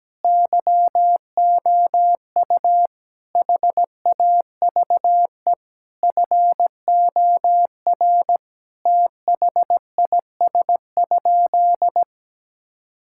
Morse code, heard as a keyed tone, about 17 wpm, 700 Hz. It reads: YOU HAVE FOR THIS?
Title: Sentences from Top 1000 Words - Review Part 5 - 17wpm